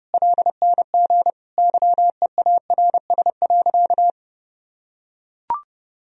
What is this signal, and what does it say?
Morse code, heard as a keyed tone, about 32 wpm, 700 Hz. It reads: LNG YEARS.
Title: Sentences from Top 200 Words and Abbreviations 30wpm